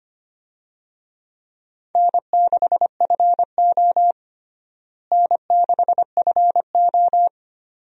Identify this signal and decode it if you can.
Morse code, heard as a keyed tone, about 25 wpm, 700 Hz. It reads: N6FO N6FO
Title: USA Call Signs - 4 Characters - Repeated 2x - 25wpm